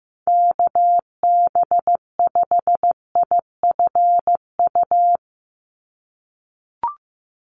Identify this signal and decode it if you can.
Morse code, heard as a keyed tone, about 16 wpm, 700 Hz. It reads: KB5IFU
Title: USA Call Signs 15wpm